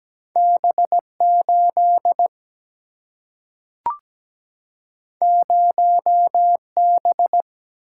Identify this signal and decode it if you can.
Morse code, heard as a keyed tone, about 17 wpm, 700 Hz. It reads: B 8 0 B
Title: Single-Letter Single-Number 17wpm